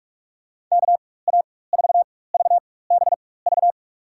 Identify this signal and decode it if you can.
Morse code, heard as a keyed tone, about 46 wpm, spaced as for 21 wpm, 700 Hz. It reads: KA4VBV
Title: USA Call Signs - 6 Characters - 4x Character Spacing - 45wpm